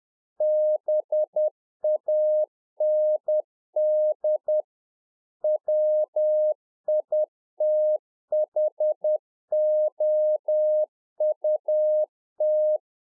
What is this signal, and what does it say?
Morse code, heard as a keyed tone, about 10 wpm, 610 Hz. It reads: BAND WITHOUT
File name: 2011-04-mp3 / UNFOGETTABLE QSO 10WPM